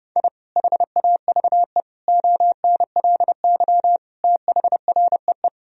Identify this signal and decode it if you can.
Morse code, heard as a keyed tone, about 30 wpm, 700 Hz. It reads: I HAVE ONLY THREE
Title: Sentences from Top 1000 Words - Review Part 1 - 30wpm